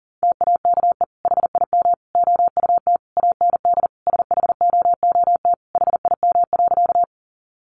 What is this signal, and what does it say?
Morse code, heard as a keyed tone, about 40 wpm, 700 Hz. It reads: TAKE HIM OUT AND SHOOT HIM.
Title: Sentences from Top 1200 Words 40wpm